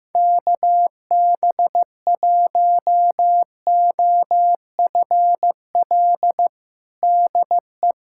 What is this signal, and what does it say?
Morse code, heard as a keyed tone, about 16 wpm, 700 Hz. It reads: KB1OFL DE
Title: USA Callsign DE Callsign - 6 Characters - 15wpm